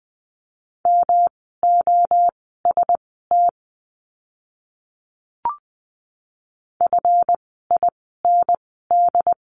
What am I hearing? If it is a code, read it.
Morse code, heard as a keyed tone, about 20 wpm, spaced as for 15 wpm, 700 Hz. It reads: MOST FIND
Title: 139 Lesson 23f - Letter f - Words 20wpm